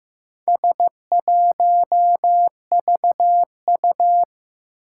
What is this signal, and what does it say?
Morse code, heard as a keyed tone, about 15 wpm, 700 Hz. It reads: S1VU
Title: Four Challenging Letter-Number Combinations 15wpm